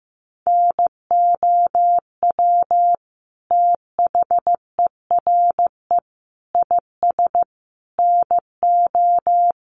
Morse code, heard as a keyed tone, about 15 wpm, 700 Hz. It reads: NOW THERE IS NO
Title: Sentences from Top 100 Words 15wpm